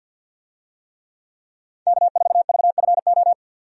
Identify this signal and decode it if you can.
Morse code, heard as a keyed tone, about 51 wpm, 700 Hz. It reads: K4VFO
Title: USA Call Signs - 5 Characters - 50wpm